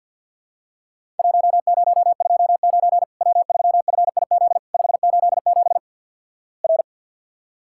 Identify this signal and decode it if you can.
Morse code, heard as a keyed tone, about 48 wpm, 700 Hz. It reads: 1029 W3FIZ 587 R
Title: Worked All Europe DX Contest - QTC Exchange - 50wpm